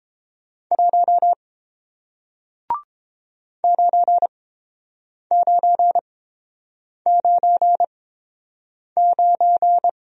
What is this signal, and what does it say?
Morse code, heard as a keyed tone, about 31 wpm, spaced as for 7 wpm, 700 Hz. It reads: 1 9999